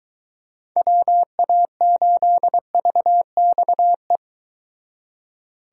Morse code, heard as a keyed tone, about 22 wpm, 700 Hz. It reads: WA8VXE